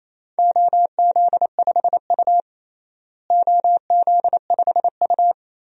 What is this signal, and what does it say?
Morse code, heard as a keyed tone, about 28 wpm, 700 Hz. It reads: OZ5U OZ5U
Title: DXCC-100 Call Signs - 4 Characters - Repeated 3x - 28wpm